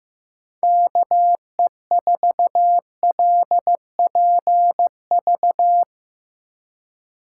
Morse code, heard as a keyed tone, about 15 wpm, 700 Hz. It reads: KE4LPV
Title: USA Call Signs - 3x Character Spacing - 15wpm